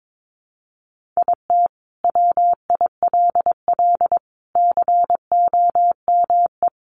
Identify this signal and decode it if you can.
Morse code, heard as a keyed tone, about 22 wpm, 700 Hz. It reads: IT WILL COME
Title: Sentences from Top 1400 Words 22wpm